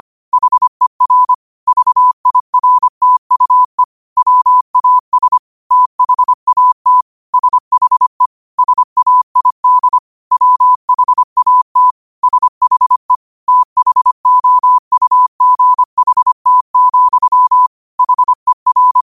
Morse code, heard as a keyed tone, about 25 wpm, 1000 Hz. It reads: HER VIRTUE WAS THAT SHE SAID WHAT SHE THOUGHT, HER
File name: qotdfiles / 25wpm